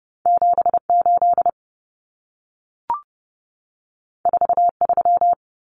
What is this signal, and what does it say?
Morse code, heard as a keyed tone, about 30 wpm, 700 Hz. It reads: 78 43
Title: Common Cut Numbers - 2-Digit 30wpm